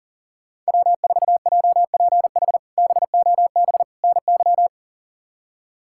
Morse code, heard as a keyed tone, about 40 wpm, 700 Hz. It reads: W4JPH BOB NY
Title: K1USN SST QSO Practice - 40wpm